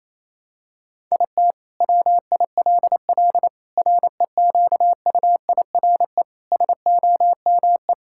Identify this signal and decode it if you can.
Morse code, heard as a keyed tone, about 28 wpm, 700 Hz. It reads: IT WILL REQUIRE SOME
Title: Sentences from Top 1000 Words - Review Part 6 - 28wpm